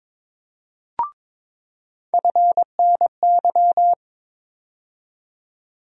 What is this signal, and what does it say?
Morse code, heard as a keyed tone, about 21 wpm, 700 Hz. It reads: FNY
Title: USA Call Sign Postfixes - 3 Letters 22wpm